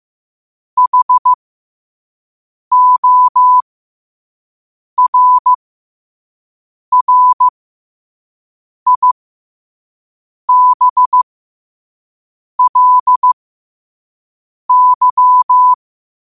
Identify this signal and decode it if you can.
Morse code, heard as a keyed tone, about 15 wpm, spaced as for 5 wpm, 1000 Hz. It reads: HORRIBLY
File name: qotdfiles / 5wpm